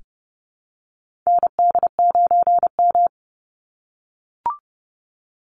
Morse code, heard as a keyed tone, about 32 wpm, 700 Hz. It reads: ND9M
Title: Call Signs - 30wpm